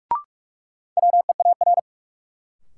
Morse code, heard as a keyed tone, about 41 wpm, 700 Hz. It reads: WEAR
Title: Sentences from Top 1000 Words - Review Part 6 - New Word Drill - 45wpm